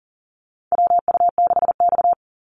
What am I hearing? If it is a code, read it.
Morse code, heard as a keyed tone, about 40 wpm, 700 Hz. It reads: WU6X
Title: Call Signs Repeated 2x - 40wpm